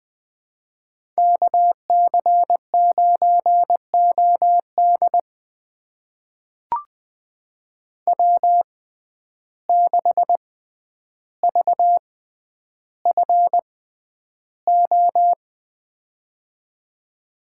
Morse code, heard as a keyed tone, about 20 wpm, spaced as for 10 wpm, 700 Hz. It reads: KC9OD W6VFO
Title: USA Call Signs - 5 Characters - 6x Character Spacing - 20wpm